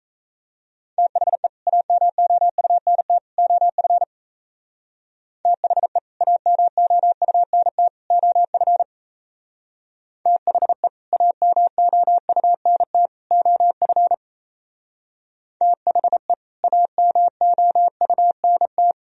Morse code, heard as a keyed tone, about 34 wpm, 700 Hz. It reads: THE AMOUNT OF THE AMOUNT OF THE AMOUNT OF THE AMOUNT
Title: Speed-Racing - Sets of 3 Words 28wpm